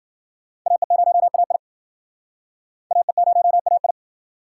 Morse code, heard as a keyed tone, about 52 wpm, 700 Hz. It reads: AE0AI AE0AI